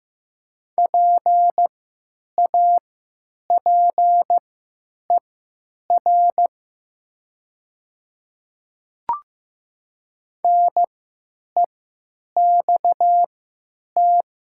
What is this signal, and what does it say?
Morse code, heard as a keyed tone, about 15 wpm, spaced as for 9 wpm, 700 Hz. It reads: PAPER NEXT
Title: 3x Character Spacing - Top 300 Words - 15wpm